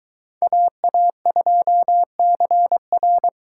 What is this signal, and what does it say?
Morse code, heard as a keyed tone, about 23 wpm, 700 Hz. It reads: AA2CR